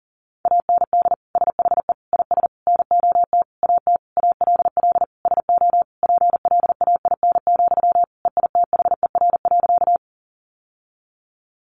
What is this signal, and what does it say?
Morse code, heard as a keyed tone, about 40 wpm, 700 Hz. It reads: AND SHE IS NOT AT ALL SO PLAIN, EITHER.
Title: Sentences from Top 600 Words 40wpm